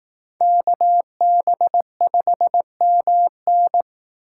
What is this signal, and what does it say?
Morse code, heard as a keyed tone, about 19 wpm, 700 Hz. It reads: KB5MN
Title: Speed Racing - USA 2x2 Call Signs 15wpm